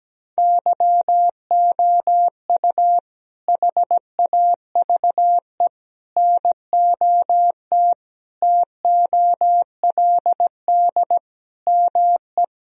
Morse code, heard as a keyed tone, about 17 wpm, 700 Hz. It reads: YOU HAVE NOT TOLD ME
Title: Sentences from Top 1000 Words - Review Part 2 - 17wpm